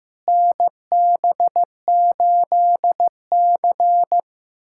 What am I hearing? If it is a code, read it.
Morse code, heard as a keyed tone, about 15 wpm, 700 Hz. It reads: NB8C
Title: USA Callsign DE Callsign - 4 Characters - 15wpm